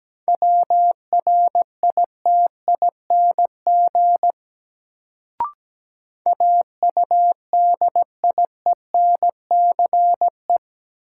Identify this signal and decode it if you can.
Morse code, heard as a keyed tone, about 17 wpm, 700 Hz. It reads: WRITING AUDIENCE
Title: Sentences from Top 800 Words - New Word Drill - 17wpm